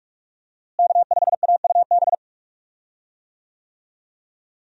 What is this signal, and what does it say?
Morse code, heard as a keyed tone, about 44 wpm, 700 Hz. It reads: K5AUB
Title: USA Call Signs - 5 Characters - Repeated 2x - 45wpm